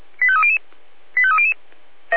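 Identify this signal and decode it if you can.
Five-tone selective call (ZVEI), heard as quick runs of five tones, tones of about 70 ms. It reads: 85300 85300
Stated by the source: Five-tone ZVEI selective call received on SDR.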